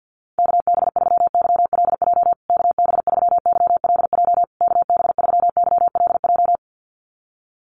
Morse code, heard as a keyed tone, about 50 wpm, 700 Hz. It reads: KB3YLJ KB3YLJ KB3YLJ
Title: USA Call Signs Repeated 3x 50wpm